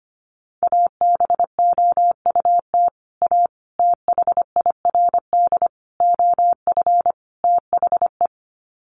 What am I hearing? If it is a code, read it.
Morse code, heard as a keyed tone, about 25 wpm, 700 Hz. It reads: ABOUT A THIRD OF THE
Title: Sets of 5 Words Encore2 25wpm